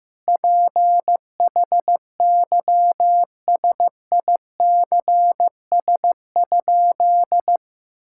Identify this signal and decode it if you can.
Morse code, heard as a keyed tone, about 15 wpm, 700 Hz. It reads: PHYSICS?
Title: Sentences from Top 1000 Words - Review Part 6 - 15wpm